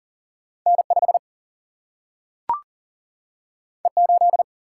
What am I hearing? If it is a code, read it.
Morse code, heard as a keyed tone, about 40 wpm, 700 Hz. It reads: N5 E8
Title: Single-Letter Single-Number 40wpm